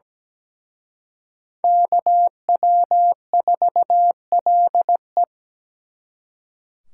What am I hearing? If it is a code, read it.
Morse code, heard as a keyed tone, about 20 wpm, 700 Hz. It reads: KW4LE
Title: Speed Racing - USA 2x2 Call Signs 17wpm